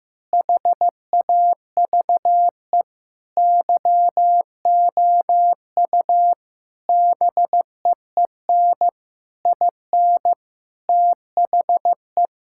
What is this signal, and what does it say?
Morse code, heard as a keyed tone, about 16 wpm, 700 Hz. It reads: HAVE YOU BEEN IN THE
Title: Sentences from Top 1000 Words - Review Part 6 - 15wpm